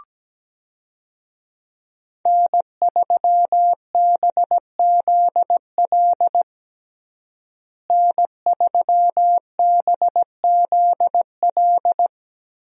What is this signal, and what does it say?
Morse code, heard as a keyed tone, about 17 wpm, 700 Hz. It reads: N3BZL N3BZL
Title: USA Call Signs - 5 Characters - Repeated 3x - 17wpm